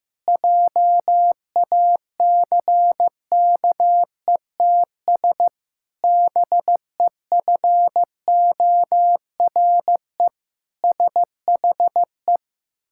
Morse code, heard as a keyed tone, about 16 wpm, 700 Hz. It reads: JACKETS BEFORE SHE
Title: Pangram Challenge v2 - 15wpm